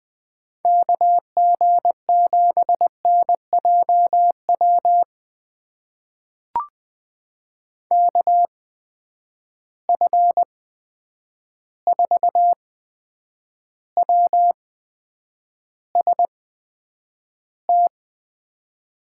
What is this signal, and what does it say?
Morse code, heard as a keyed tone, about 20 wpm, spaced as for 9 wpm, 700 Hz. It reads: KG7NJW KF4WST
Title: USA Call Signs - 6 Characters - 8x Character Spacing - 20wpm